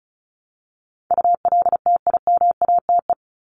Morse code, heard as a keyed tone, about 34 wpm, 700 Hz. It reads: ULTIMATE
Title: LY Suffix - Top 100 Words with Base Words 35wpm